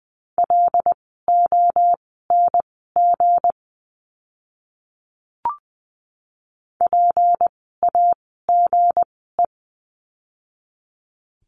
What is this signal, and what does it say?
Morse code, heard as a keyed tone, about 20 wpm, spaced as for 14 wpm, 700 Hz. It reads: LONG PAGE
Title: 160 Lesson 26f - Letter g - Words 20wpm